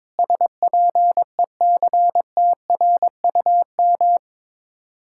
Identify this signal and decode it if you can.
Morse code, heard as a keyed tone, about 22 wpm, 700 Hz. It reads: SPECTRUM
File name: practice / 22wpm / Sets of 7 Words - 22wpm